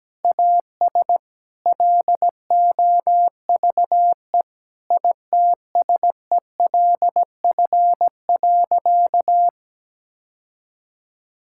Morse code, heard as a keyed tone, about 17 wpm, 700 Hz. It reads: AS LOVE ITSELF.
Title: Sentences from Top 1000 Words - Review Part 5 - 17wpm